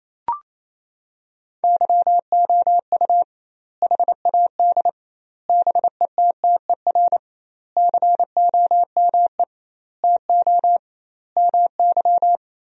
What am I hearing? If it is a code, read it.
Morse code, heard as a keyed tone, about 28 wpm, 700 Hz. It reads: YOU HAD BETTER COME TO MY
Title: 2x Word-Spacing - Sentences from Top 400 Words - 28wpm